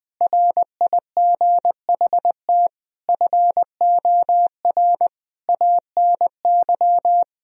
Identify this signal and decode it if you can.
Morse code, heard as a keyed tone, about 21 wpm, 700 Hz. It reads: RIGHT FOR ANY
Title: Sentences from Top 1000 Words - Review Part 2 - 20wpm